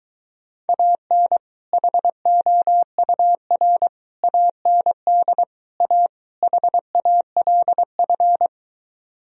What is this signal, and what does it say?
Morse code, heard as a keyed tone, about 23 wpm, 700 Hz. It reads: AN HOUR AND A HALF